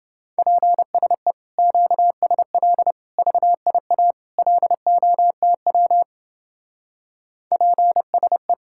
Morse code, heard as a keyed tone, about 30 wpm, 700 Hz. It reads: PSE QSL VIA LOTW PSE